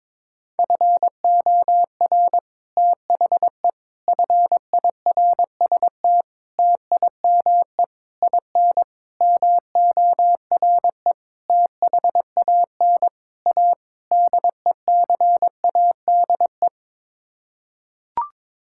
Morse code, heard as a keyed tone, about 22 wpm, 700 Hz. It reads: FOR THE FIRST TIME IN MORE THAN A DECADE
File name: Sets of 9 Words - 22wpm